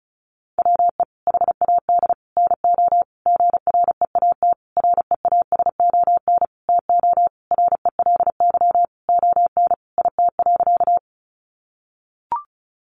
Morse code, heard as a keyed tone, about 34 wpm, 700 Hz. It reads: WE HAD NO GREAT REASON TO RELY ON IT.
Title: Sentences from Top 1200 Words 35wpm